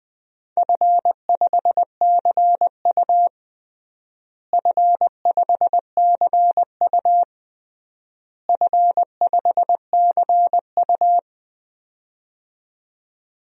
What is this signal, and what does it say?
Morse code, heard as a keyed tone, about 21 wpm, 700 Hz. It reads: F5CU F5CU F5CU
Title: DXCC-100 Call Signs - 4 Characters - Repeated 3x - 20wpm